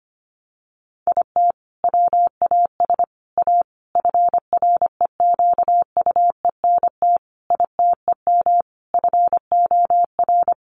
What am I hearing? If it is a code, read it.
Morse code, heard as a keyed tone, about 25 wpm, 700 Hz. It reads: IT WAS A FREQUENT ITEM FOR
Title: Sentences from Top 1400 Words 25wpm